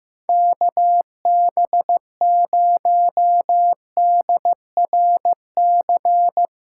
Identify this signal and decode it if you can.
Morse code, heard as a keyed tone, about 16 wpm, 700 Hz. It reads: KB0DRC
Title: USA Callsign DE Callsign - 6 Characters - 15wpm